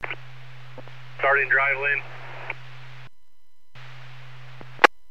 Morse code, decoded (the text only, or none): none